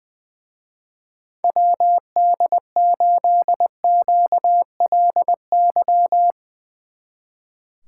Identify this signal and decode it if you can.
Morse code, heard as a keyed tone, about 22 wpm, 700 Hz. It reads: WD8QLY
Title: Speed Racing - USA 2x3 Call Signs 20wpm